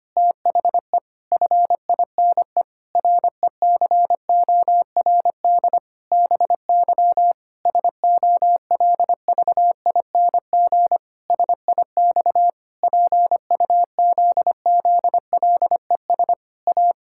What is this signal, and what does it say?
Morse code, heard as a keyed tone, about 25 wpm, 700 Hz. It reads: THE FINE RECORD BY SOLVING SIX PUZZLES A
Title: Pangram Challenge v2 - 25wpm